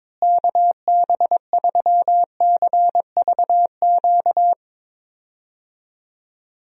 Morse code, heard as a keyed tone, about 22 wpm, 700 Hz. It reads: KB3CVQ
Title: USA Call Signs - 6 Characters - Repeated 3x - 22wpm